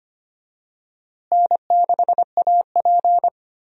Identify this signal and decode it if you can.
Morse code, heard as a keyed tone, about 25 wpm, 700 Hz. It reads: N6AP
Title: USA Call Signs - 4 Characters - Repeated 2x - 25wpm